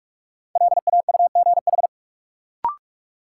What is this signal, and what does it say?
Morse code, heard as a keyed tone, about 42 wpm, 700 Hz. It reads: LAUGH
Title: Sentences from Top 500 Words - New Word Drill - 45wpm